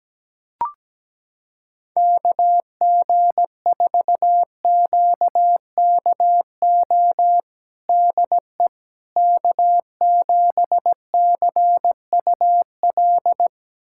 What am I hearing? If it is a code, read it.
Morse code, heard as a keyed tone, about 17 wpm, 700 Hz. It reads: KG4QKO DE K7CUL
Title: USA Callsign DE Callsign 17wpm